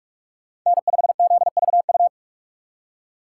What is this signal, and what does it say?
Morse code, heard as a keyed tone, about 54 wpm, 700 Hz. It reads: N5ZVU
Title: Speed Racing - USA 1x3 Call Signs 45wpm